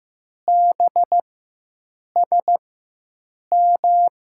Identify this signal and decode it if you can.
Morse code, heard as a keyed tone, about 15 wpm, spaced as for 7 wpm, 700 Hz. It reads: BSM
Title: USA Call Signs - 6 Characters - 4x Character Spacing - 15wpm